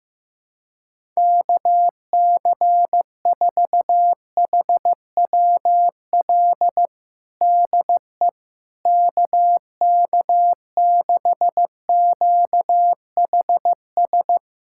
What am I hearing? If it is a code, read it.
Morse code, heard as a keyed tone, about 15 wpm, 700 Hz. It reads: KC4HWL DE KK6QHS
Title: USA Callsign DE Callsign 15wpm